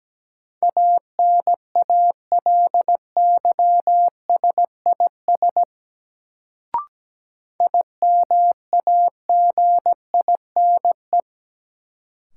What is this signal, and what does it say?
Morse code, heard as a keyed tone, about 17 wpm, 700 Hz. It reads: ANALYSIS IMAGINE
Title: Sentences from Top 1400 Words - New Word Drill Alt - 17wpm